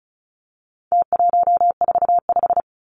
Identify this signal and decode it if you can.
Morse code, heard as a keyed tone, about 34 wpm, 700 Hz. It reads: T145
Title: Numbers - Contest Style - 4-Digit 35wpm